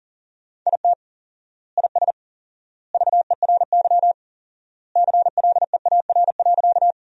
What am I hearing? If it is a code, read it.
Morse code, heard as a keyed tone, about 37 wpm, 700 Hz. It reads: IT IS VERY CLEAR.
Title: 4x Word-Spacing - Sentences from Top 500 Words - 40wpm